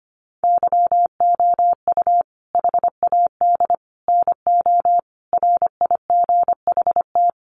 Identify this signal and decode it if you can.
Morse code, heard as a keyed tone, about 25 wpm, 700 Hz. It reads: YOU HAD NO RIGHT
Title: Sentences from Top 1300 Words 25wpm